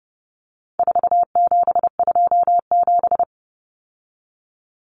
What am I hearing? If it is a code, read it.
Morse code, heard as a keyed tone, about 30 wpm, 700 Hz. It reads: 4727
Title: Numbers - 4-Digit 30wpm